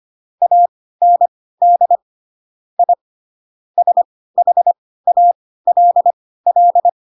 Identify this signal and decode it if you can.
Morse code, heard as a keyed tone, about 25 wpm, spaced as for 16 wpm, 700 Hz. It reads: AND I SHALL
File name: Pinocchio / Pinocchio ch7 10 25 ik1hge